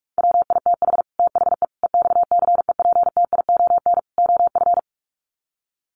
Morse code, heard as a keyed tone, about 44 wpm, 700 Hz. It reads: WITH THE EXCEPTION OF
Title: Sets of 4 Words 45wpm